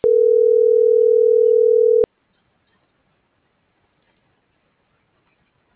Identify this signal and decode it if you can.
Ringback tone (North America), 440 + 480 Hz.